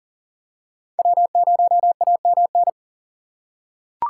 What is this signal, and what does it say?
Morse code, heard as a keyed tone, about 40 wpm, 700 Hz. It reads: W0AMN
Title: USA Call Signs - 5 Characters - 3x Character Spacing - 40wpm